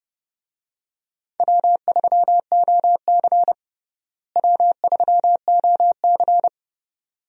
Morse code, heard as a keyed tone, about 32 wpm, 700 Hz. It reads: W3OC W3OC
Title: USA Call Signs - 4 Characters - Repeated 2x - 30wpm